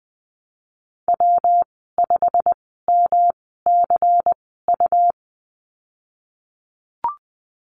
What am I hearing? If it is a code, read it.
Morse code, heard as a keyed tone, about 20 wpm, spaced as for 15 wpm, 700 Hz. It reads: W5MCU
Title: 167 Lesson 27g - Number 7 - Call Signs 20wpm